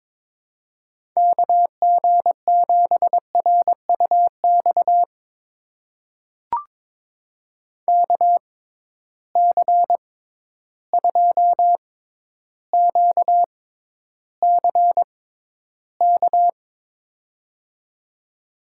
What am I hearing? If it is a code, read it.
Morse code, heard as a keyed tone, about 22 wpm, spaced as for 11 wpm, 700 Hz. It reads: KG7RUX KC2QCK
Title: USA Call Signs - 6 Characters - 6x Character Spacing - 22wpm